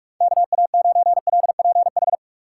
Morse code, heard as a keyed tone, about 46 wpm, 700 Hz. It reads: KA9LPH
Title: USA Call Signs - 6 Characters - 3x Character Spacing - 45wpm